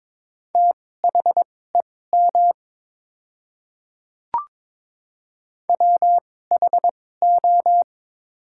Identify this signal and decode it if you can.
Morse code, heard as a keyed tone, about 22 wpm, spaced as for 16 wpm, 700 Hz. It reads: THEM WHO
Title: 2x Character Spacing - Top 100 Words - 22wpm